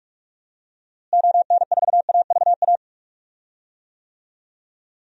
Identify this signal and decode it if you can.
Morse code, heard as a keyed tone, about 42 wpm, 700 Hz. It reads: ON4AVA
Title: DXCC-100 Call Signs - 6 Characters - Repeated 3x - 45wpm